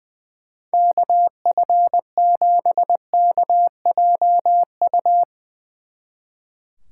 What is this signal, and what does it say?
Morse code, heard as a keyed tone, about 22 wpm, 700 Hz. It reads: KF7KJU
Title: Speed Racing - USA Call Signs 20wpm